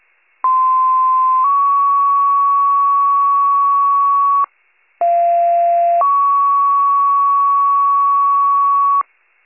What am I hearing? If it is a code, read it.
Two-tone page, heard as a one-second tone, then a three-second tone, about 1040 then 1120 Hz, about 690 then 1120 Hz. It is heard from 0.4 to 9.0 s.